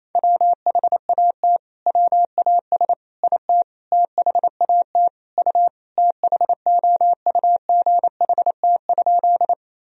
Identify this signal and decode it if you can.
Morse code, heard as a keyed tone, about 28 wpm, 700 Hz. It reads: WHAT WAS IT THAT U THOUGHT?
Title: Sentences from Top 300 Words and Abbreviations 28wpm